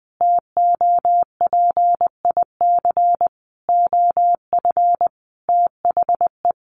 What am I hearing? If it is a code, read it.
Morse code, heard as a keyed tone, about 20 wpm, 700 Hz. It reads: TOPIC OF THE
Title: Sentences from Top 1200 Words 20wpm